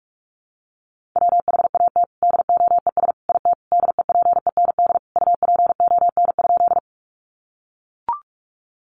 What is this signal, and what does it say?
Morse code, heard as a keyed tone, about 44 wpm, 700 Hz. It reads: WHAT DOES IT DEPEND UPON?
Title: Sentences from Top 1300 Words 45wpm